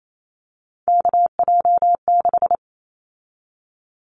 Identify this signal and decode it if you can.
Morse code, heard as a keyed tone, about 28 wpm, 700 Hz. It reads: KJ6
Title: Callsign Prefixes for Top 25 DXCC - 28wpm